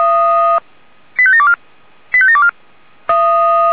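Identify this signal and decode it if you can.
Five-tone selective call (ZVEI), heard as quick runs of five tones, tones of about 70 ms. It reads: 85724 85724
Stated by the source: ZVEI five-tone call received by SDR.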